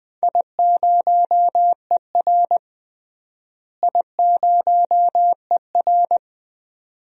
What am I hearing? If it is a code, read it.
Morse code, heard as a keyed tone, about 20 wpm, 700 Hz. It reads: I0ER I0ER
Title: DXCC-100 Call Signs - 4 Characters - Repeated 2x - 20wpm